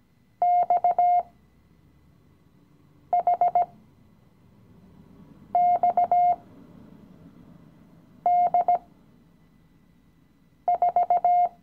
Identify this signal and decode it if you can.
Morse code, heard as a keyed tone, about 18 wpm, spaced as for 4 wpm, 710 Hz. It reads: XHXD4
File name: CW-Online-Kurs / Anfänger / Lektion 016